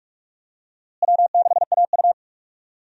Morse code, heard as a keyed tone, about 43 wpm, 700 Hz. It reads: W6AU
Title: USA Call Signs - 4 Characters - Repeated 2x - 45wpm